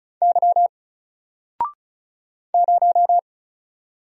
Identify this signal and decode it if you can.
Morse code, heard as a keyed tone, about 36 wpm, 700 Hz. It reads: Y 0
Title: Single Letter-Number Alt 35wpm